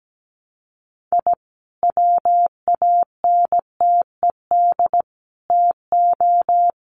Morse code, heard as a keyed tone, about 17 wpm, 700 Hz. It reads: I WANTED TO
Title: Sets of 5 Words Encore2 17wpm